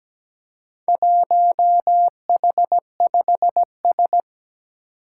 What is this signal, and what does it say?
Morse code, heard as a keyed tone, about 17 wpm, 700 Hz. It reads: 1H5S
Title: Four Challenging Letter-Number Combinations 17wpm